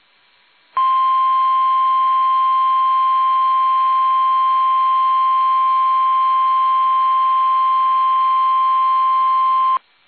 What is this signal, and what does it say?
Emergency Alert System (SAME) tones (the 1050 Hz attention tone). It is heard from 0.7 to 9.8 s.
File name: eas / KUSO / KUSO